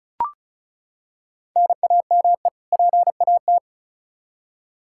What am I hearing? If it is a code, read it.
Morse code, heard as a keyed tone, about 34 wpm, 700 Hz. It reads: NAME PAT
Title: Practice QSO - Exchange 1 - Beginning - 35wpm